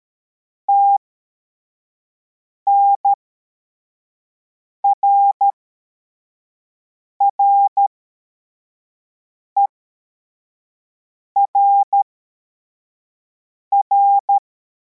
Morse code, heard as a keyed tone, about 13 wpm, spaced as for 4 wpm, 800 Hz. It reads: TNRRERR